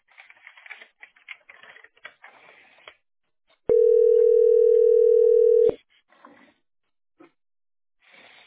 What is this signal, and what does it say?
Ringback tone (North America), 440 + 480 Hz.
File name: be887f4a-0bf2-4b38-b68b